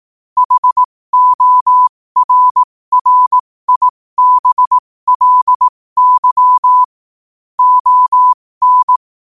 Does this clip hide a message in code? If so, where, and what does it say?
Yes: Morse code from 0.4 to 9.0 s, heard as a keyed tone, about 18 wpm, spaced as for 15 wpm, 1000 Hz. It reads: HORRIBLY ON